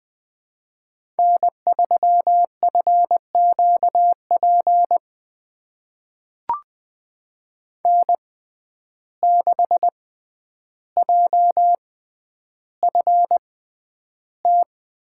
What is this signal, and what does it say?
Morse code, heard as a keyed tone, about 20 wpm, spaced as for 10 wpm, 700 Hz. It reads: N3FQP N6JFT
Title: USA Call Signs - 5 Characters - 6x Character Spacing - 20wpm